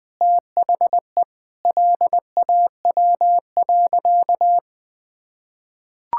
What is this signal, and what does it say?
Morse code, heard as a keyed tone, about 19 wpm, 700 Hz. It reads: THE LAW.
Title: Sentences with 7-Word Phrases - Spoken Beginning - 20wpm